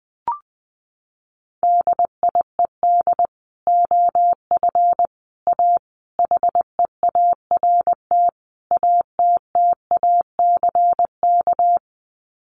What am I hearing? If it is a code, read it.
Morse code, heard as a keyed tone, about 20 wpm, 700 Hz. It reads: DIED OF A HEART ATTACK
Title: Sets of 5 Words Encore2 20wpm